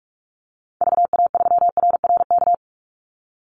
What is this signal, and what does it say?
Morse code, heard as a keyed tone, about 44 wpm, 700 Hz. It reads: VA3RRK
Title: Call Signs - 45wpm